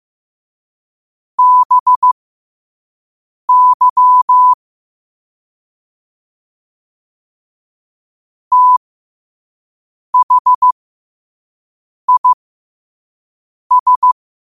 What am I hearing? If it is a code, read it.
Morse code, heard as a keyed tone, about 15 wpm, spaced as for 5 wpm, 1000 Hz. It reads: BY THIS
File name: qotdfiles / 5wpm